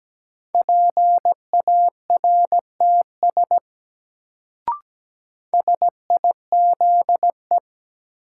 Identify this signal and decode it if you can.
Morse code, heard as a keyed tone, about 17 wpm, 700 Hz. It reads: PARTS SIZE
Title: Sets of 3 Words - New Word Drill - 17wpm